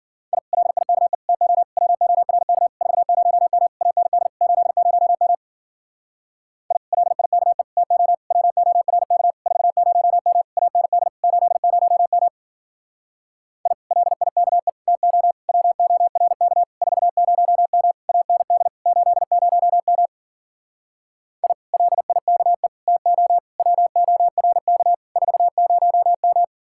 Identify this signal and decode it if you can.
Morse code, heard as a keyed tone, about 48 wpm, 700 Hz. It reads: I LIKE TO WORK 40M AND 80M I LIKE TO WORK 40M AND 80M I LIKE TO WORK 40M AND 80M I LIKE TO WORK 40M
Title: Speed-Racing Common QSO Phrases - 40wpm